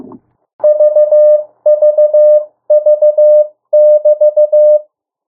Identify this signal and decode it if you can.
Morse code, heard as a keyed tone, about 15 wpm, 600 Hz. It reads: VVV<BT>
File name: morse mp3 / 02 urok 1